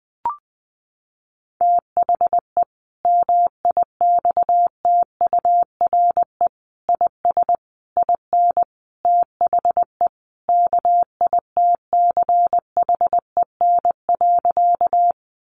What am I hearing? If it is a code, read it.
Morse code, heard as a keyed tone, about 20 wpm, 700 Hz. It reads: THE MIXTURE IS IN THE KITCHEN.
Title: Sentences from Top 1500 Words 20wpm